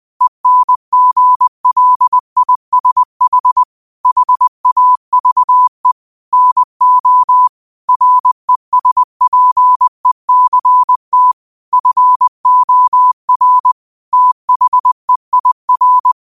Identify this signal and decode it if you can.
Morse code, heard as a keyed tone, about 20 wpm, 1000 Hz. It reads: ENGLISH HAVE NO RESPECT FOR THEIR